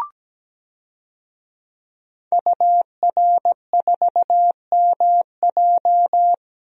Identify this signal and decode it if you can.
Morse code, heard as a keyed tone, about 17 wpm, 700 Hz. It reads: UR4MJ
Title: DXCC-100 Call Signs - 5 Characters - Repeated 2x - 17wpm